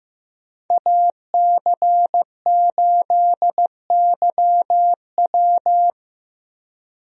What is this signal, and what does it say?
Morse code, heard as a keyed tone, about 15 wpm, 700 Hz. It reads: AC8YW
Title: USA Call Signs - 5 Characters - Repeated 2x - 15wpm